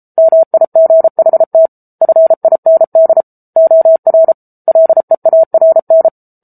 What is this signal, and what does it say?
Morse code, heard as a keyed tone, about 34 wpm, 650 Hz. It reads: MIGHT FIND OR LEARN
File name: Book Two / 07-Chapter 7 - 33 WPM